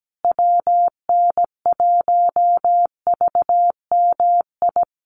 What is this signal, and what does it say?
Morse code, heard as a keyed tone, about 17 wpm, 700 Hz. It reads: WN1VMI
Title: USA Call Signs Repeated 3x 17wpm